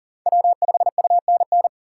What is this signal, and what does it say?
Morse code, heard as a keyed tone, about 40 wpm, 700 Hz. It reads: W5UNN
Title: USA Call Signs - 5 Characters - 3x Character Spacing - 40wpm